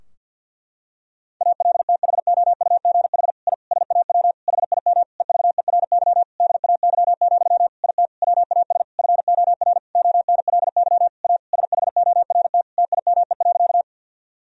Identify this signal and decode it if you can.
Morse code, heard as a keyed tone, about 51 wpm, 700 Hz. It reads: ALTHOUGH I SAW HIM EVERY DAY, IT WAS FOR ONLY A SHORT TIME.
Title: Sentences from Top 1000 Words - Review Part 3 - 50wpm